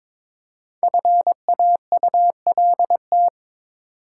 Morse code, heard as a keyed tone, about 22 wpm, 700 Hz. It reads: FAULT
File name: 5 Letter Words - First 3 Letters Given - 22wpm